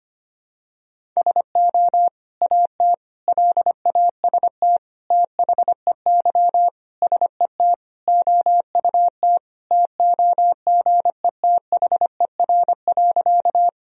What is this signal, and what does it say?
Morse code, heard as a keyed tone, about 25 wpm, 700 Hz. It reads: SO AT LAST THEY SET OUT TOGETHER.
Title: Sentences from Top 1000 Words - Review Part 1 - 25wpm